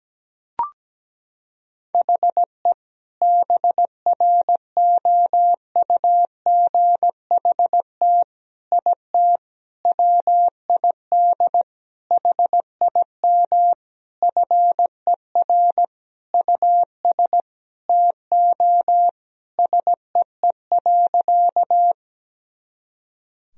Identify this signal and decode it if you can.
Morse code, heard as a keyed tone, about 17 wpm, 700 Hz. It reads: HE BROUGHT IT WID HIM FER US TO SEE.
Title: Sentences from Top 500 Words and Abbreviations 17wpm